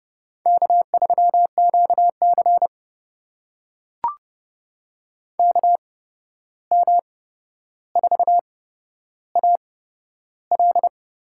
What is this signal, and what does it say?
Morse code, heard as a keyed tone, about 30 wpm, spaced as for 12 wpm, 700 Hz. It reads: K3QC KM4AL
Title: USA Call Signs - 8x Character Spacing - 30wpm